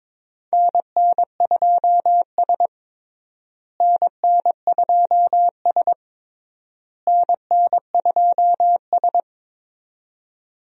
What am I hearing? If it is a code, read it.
Morse code, heard as a keyed tone, about 22 wpm, 700 Hz. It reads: NN2S NN2S NN2S
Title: USA Call Signs - 4 Characters - Repeated 3x - 22wpm